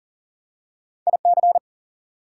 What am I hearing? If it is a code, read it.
Morse code, heard as a keyed tone, about 40 wpm, 700 Hz. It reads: IC
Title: Prefixes and Suffixes 40wpm